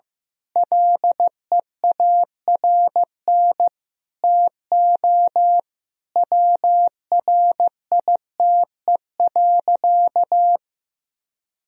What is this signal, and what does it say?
Morse code, heard as a keyed tone, about 16 wpm, 700 Hz. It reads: LEARN TO WRITE.
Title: Sentences from Top 1000 Words - Review Part 1 - 15wpm